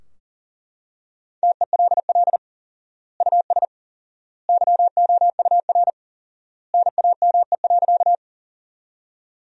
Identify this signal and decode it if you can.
Morse code, heard as a keyed tone, about 40 wpm, 700 Hz. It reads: TELL US YOUR NAME.
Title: 4x Word-Spacing - Sentences from Top 200 Words - 40wpm